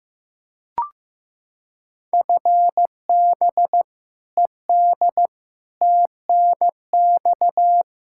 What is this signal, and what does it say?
Morse code, heard as a keyed tone, about 15 wpm, 700 Hz. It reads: FB ED TNX
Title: K1USN SST QSO Practice 15wpm